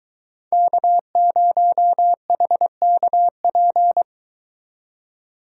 Morse code, heard as a keyed tone, about 23 wpm, 700 Hz. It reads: K0HKP